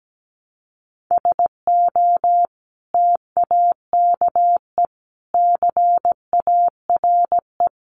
Morse code, heard as a keyed tone, about 17 wpm, 700 Hz. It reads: SO TAKE CARE
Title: Sentences from Top 300 Words 17wpm